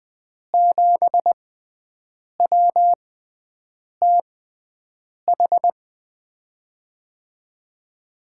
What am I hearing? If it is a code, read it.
Morse code, heard as a keyed tone, about 20 wpm, spaced as for 7 wpm, 700 Hz. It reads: 7WTH
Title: USA Call Signs - 6 Characters - 6x Character Spacing - 20wpm